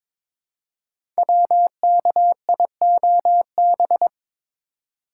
Morse code, heard as a keyed tone, about 22 wpm, 700 Hz. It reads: WKIOB
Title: Code Groups - 5 Letters - 22wpm